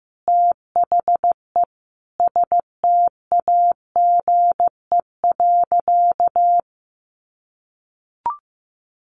Morse code, heard as a keyed tone, about 15 wpm, 700 Hz. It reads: THE STAGE.
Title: Sentences from Top 1200 Words 15wpm